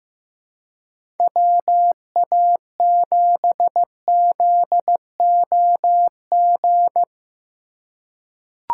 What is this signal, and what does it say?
Morse code, heard as a keyed tone, about 15 wpm, 700 Hz. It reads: WA7ZOG
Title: USA Call Signs - 3x Character Spacing - 15wpm